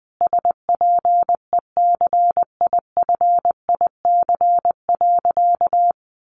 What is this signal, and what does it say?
Morse code, heard as a keyed tone, about 19 wpm, 700 Hz. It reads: SPECIFIC.
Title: Sentences from Top 1300 Words 20wpm